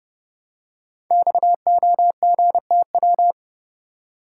Morse code, heard as a keyed tone, about 30 wpm, 700 Hz. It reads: XOGTW
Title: Code Groups - 5 Letter:Numbers - 30wpm